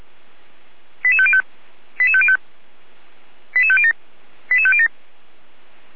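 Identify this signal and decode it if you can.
Five-tone selective call (ZVEI), heard as quick runs of five tones, tones of about 70 ms. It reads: 88505 88505 88507 88507
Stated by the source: ZVEI five-tone call 88405 received by SDR.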